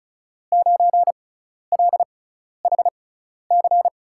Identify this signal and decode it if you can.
Morse code, heard as a keyed tone, about 36 wpm, spaced as for 12 wpm, 700 Hz. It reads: 9LHC
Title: USA Call Signs - 5 Characters - 6x Character Spacing - 35wpm